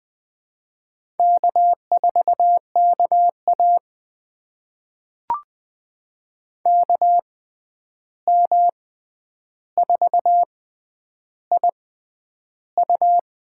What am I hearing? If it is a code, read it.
Morse code, heard as a keyed tone, about 20 wpm, spaced as for 10 wpm, 700 Hz. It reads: K4KA KM4IU
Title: USA Call Signs - 6x Character Spacing - 20wpm